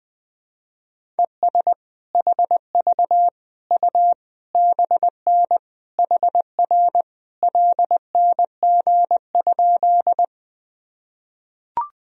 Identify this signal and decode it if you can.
Morse code, heard as a keyed tone, about 20 wpm, 700 Hz. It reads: ES HV U BN HR LNG?
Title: Sentences from Top 200 Words and Abbreviations 20wpm